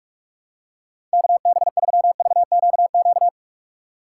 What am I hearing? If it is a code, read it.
Morse code, heard as a keyed tone, about 46 wpm, 700 Hz. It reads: KB3VQQ
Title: USA Call Signs - 6 Characters - 3x Character Spacing - 45wpm